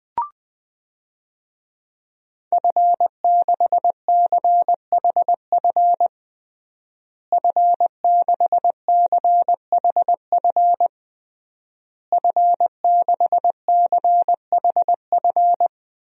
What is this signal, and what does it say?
Morse code, heard as a keyed tone, about 20 wpm, 700 Hz. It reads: F6CHF F6CHF F6CHF
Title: DXCC-100 Call Signs - 5 Characters - Repeated 3x - 20wpm